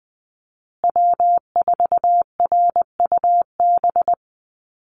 Morse code, heard as a keyed tone, about 20 wpm, 700 Hz. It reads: W4RUB
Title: USA Call Signs Repeated 2x 20wpm